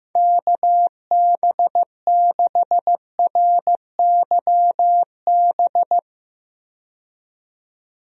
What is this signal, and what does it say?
Morse code, heard as a keyed tone, about 15 wpm, 700 Hz. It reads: KB6RYB
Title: USA Call Signs - 6 Characters - 15wpm